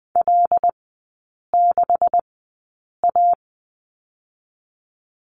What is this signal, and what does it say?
Morse code, heard as a keyed tone, about 19 wpm, spaced as for 8 wpm, 700 Hz. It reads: L6A
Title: 127 Lesson 21e - Number 6 - Three Characters 20wpm